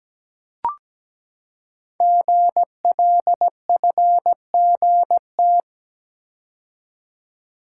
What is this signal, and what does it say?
Morse code, heard as a keyed tone, about 17 wpm, 700 Hz. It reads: GLFGT
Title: Code Groups - 5 Letters - 17wpm